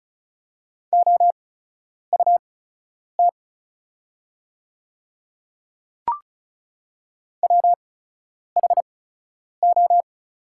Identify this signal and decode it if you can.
Morse code, heard as a keyed tone, about 36 wpm, spaced as for 10 wpm, 690 Hz. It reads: OUT WHO
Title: 8x Character Spacing - Top 100 Words - 35wpm